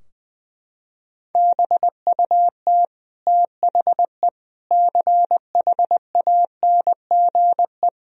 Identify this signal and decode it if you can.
Morse code, heard as a keyed tone, about 20 wpm, 700 Hz. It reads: BUT THE CHANGE
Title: Sentences from Top 1000 Words - Review Part 4 - 20wpm